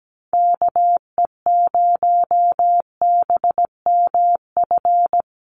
Morse code, heard as a keyed tone, about 17 wpm, 700 Hz. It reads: KE0BMF
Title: USA Call Signs 17wpm